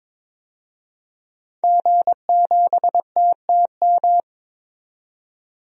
Morse code, heard as a keyed tone, about 22 wpm, 700 Hz. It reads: G7TTM
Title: DXCC-100 Call Signs - 5 Characters - 22wpm